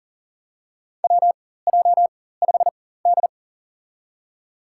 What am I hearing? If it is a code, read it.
Morse code, heard as a keyed tone, about 39 wpm, spaced as for 19 wpm, 700 Hz. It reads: WJ5D
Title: USA Call Signs - 4 Characters - 4x Character Spacing - 40wpm